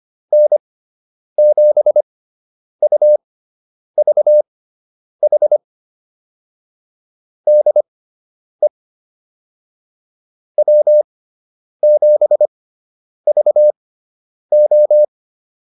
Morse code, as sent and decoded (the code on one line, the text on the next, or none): -. --... ..- ...- .... / -.. / .-- --... ...- ---
N7UVH DE W7VO